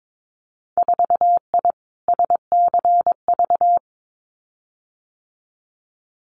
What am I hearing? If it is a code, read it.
Morse code, heard as a keyed tone, about 22 wpm, 700 Hz. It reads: 4I SCV
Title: Winter Field Day Exchanges Encore 22wpm